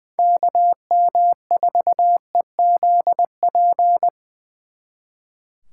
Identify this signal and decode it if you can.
Morse code, heard as a keyed tone, about 23 wpm, 700 Hz. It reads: KM4EZP
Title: Speed Racing - USA 2x3 Call Signs 20wpm